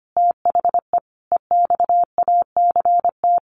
Morse code, heard as a keyed tone, about 25 wpm, 700 Hz. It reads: THE EXACT
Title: A Sound Of Thunder 25wpm